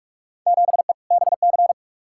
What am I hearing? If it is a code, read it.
Morse code, heard as a keyed tone, about 42 wpm, 700 Hz. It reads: 7E BC
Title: Field Day Exchanges Encore - 45wpm